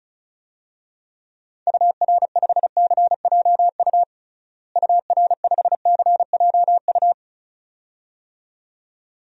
Morse code, heard as a keyed tone, about 34 wpm, 700 Hz. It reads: UR5CJU UR5CJU
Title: DXCC-100 Call Signs - 6 Characters - Repeated 2x - 35wpm